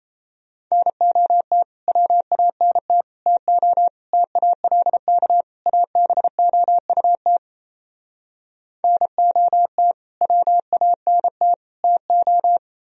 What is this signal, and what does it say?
Morse code, heard as a keyed tone, about 37 wpm, 700 Hz. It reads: NOT WANT TO TALK ABOUT NOT WANT TO